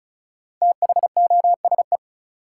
Morse code, heard as a keyed tone, about 36 wpm, 700 Hz. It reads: THOSE
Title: Sets of 2 Words - New Word Drill - 35wpm